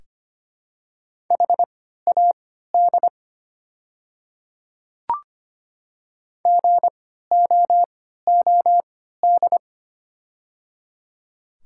Morse code, heard as a keyed tone, about 25 wpm, spaced as for 14 wpm, 700 Hz. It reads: HAD GOOD
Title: 3x Character Spacing - Top 200 Words - 25wpm